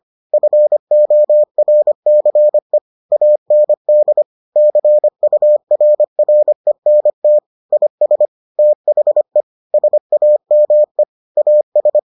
Morse code, heard as a keyed tone, about 25 wpm, 600 Hz. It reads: FORCE AND CURRENT IS THE SAME AS